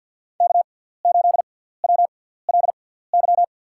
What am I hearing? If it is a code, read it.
Morse code, heard as a keyed tone, about 52 wpm, spaced as for 17 wpm, 700 Hz. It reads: K8WLY
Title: USA Call Signs - 5 Characters - 6x Character Spacing - 50wpm